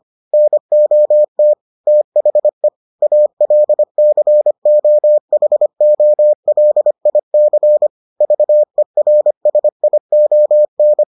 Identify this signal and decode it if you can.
Morse code, heard as a keyed tone, about 25 wpm, 600 Hz. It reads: NOT THE ALCOHOLIC VERSION